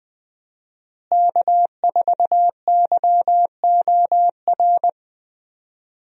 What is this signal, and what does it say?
Morse code, heard as a keyed tone, about 20 wpm, 700 Hz. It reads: K4YOR
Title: USA Call Signs - 5 Characters - Repeated 2x - 20wpm